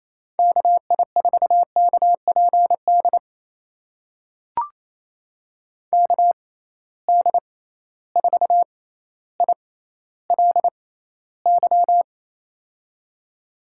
Morse code, heard as a keyed tone, about 28 wpm, spaced as for 14 wpm, 700 Hz. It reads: KI4KPD KD4ILY
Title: USA Call Signs - 6 Characters - 6x Character Spacing - 28wpm